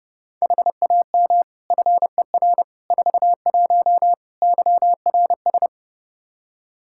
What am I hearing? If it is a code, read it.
Morse code, heard as a keyed tone, about 32 wpm, 700 Hz. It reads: HAM FER 41 YRS
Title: K1USN SST QSO Practice 30wpm